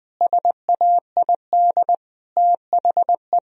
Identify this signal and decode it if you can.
Morse code, heard as a keyed tone, about 21 wpm, 700 Hz. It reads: SAID THE
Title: Sentences from Top 1000 Words - Review Part 1 - 20wpm